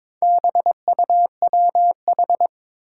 Morse code, heard as a keyed tone, about 22 wpm, 700 Hz. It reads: BUWH
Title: Four Challenging Letter-Number Combinations 22wpm